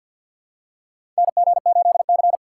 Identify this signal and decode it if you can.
Morse code, heard as a keyed tone, about 47 wpm, 700 Hz. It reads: NG8C
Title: USA Call Signs - 4 Characters - 6x Character Spacing - 50wpm